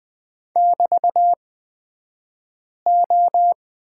Morse code, heard as a keyed tone, about 20 wpm, spaced as for 5 wpm, 700 Hz. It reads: <BT>O
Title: 223 Lesson 36d - BT - Two Characters 20wpm